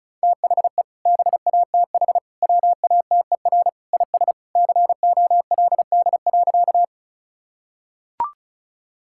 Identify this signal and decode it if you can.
Morse code, heard as a keyed tone, about 36 wpm, 700 Hz. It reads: THE BATH WATER IS COLD.
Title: Sentences from Top 1600 Words 35wpm